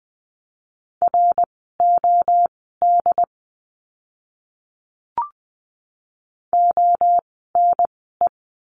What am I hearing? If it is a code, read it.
Morse code, heard as a keyed tone, about 20 wpm, spaced as for 15 wpm, 700 Hz. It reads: ROD ONE
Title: 068 Lesson 12f - Letter d - Words 20wpm